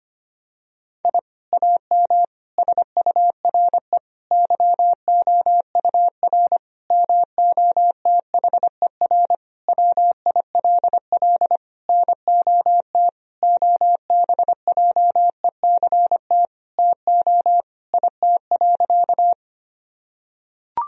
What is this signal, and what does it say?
Morse code, heard as a keyed tone, about 25 wpm, 700 Hz. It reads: I AM SURE YOUR MOTHER WILL NOT OBJECT TO IT.
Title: Sentences from Top 1000 Words - Review Part 3 - 25wpm